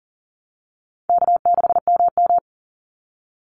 Morse code, heard as a keyed tone, about 37 wpm, 700 Hz. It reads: K6MM
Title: Easy Call Signs Repeated 3x - 40wpm